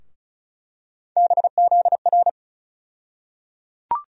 Morse code, heard as a keyed tone, about 36 wpm, 700 Hz. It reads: BZR
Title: USA Call Sign Postfixes - 3 Letters 35wpm